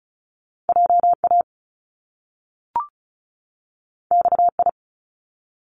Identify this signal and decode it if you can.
Morse code, heard as a keyed tone, about 36 wpm, 700 Hz. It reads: JA XI
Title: Top Two Letter Combinations in Words 35wpm